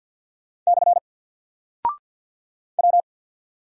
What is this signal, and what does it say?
Morse code, heard as a keyed tone, about 48 wpm, 700 Hz. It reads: / W
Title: Single Character (Letter:Number:Punctuation:Prosign) Alt 50wpm